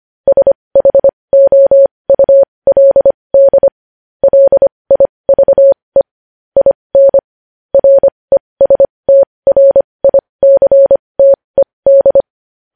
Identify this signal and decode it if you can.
Morse code, heard as a keyed tone, about 25 wpm, spaced as for 20 wpm, 550 Hz. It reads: SHOULD LIVE IN RESTRICTED
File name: podcast / News headlines-1x-25x20